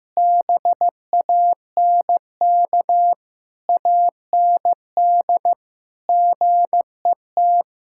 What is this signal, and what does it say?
Morse code, heard as a keyed tone, about 15 wpm, 700 Hz. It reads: BANK AND GET
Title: Sentences from Top 1000 Words - Review Part 5 - 15wpm